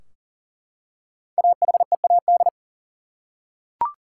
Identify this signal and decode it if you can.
Morse code, heard as a keyed tone, about 40 wpm, 700 Hz. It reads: AHEAD
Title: Five Letter Words - 40wpm